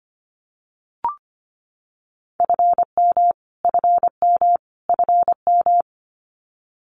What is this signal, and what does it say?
Morse code, heard as a keyed tone, about 25 wpm, 700 Hz. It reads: FM FM FM
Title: Abbreviations Repeated 3x - 25wpm